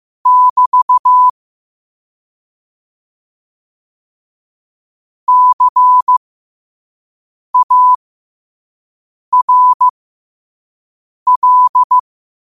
Morse code, heard as a keyed tone, about 15 wpm, spaced as for 5 wpm, 1000 Hz. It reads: <BT> CARL
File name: qotdfiles / 5wpm